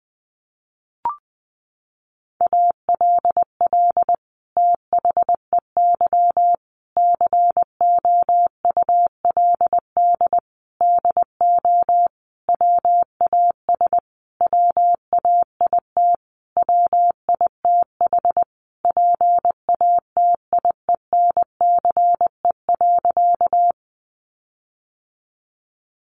Morse code, heard as a keyed tone, about 20 wpm, 700 Hz. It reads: ALL THEY COULD DO WAS WAIT WITH PATIENCE.
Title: Sentences from Top 1300 Words 20wpm